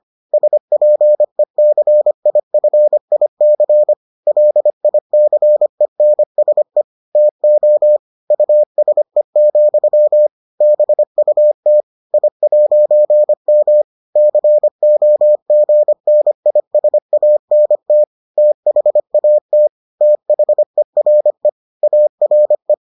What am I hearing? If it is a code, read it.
Morse code, heard as a keyed tone, about 25 wpm, 600 Hz. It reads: SPECIFIC LICENSE TO USE, BUT I'M COGNISANT THAT THERE ARE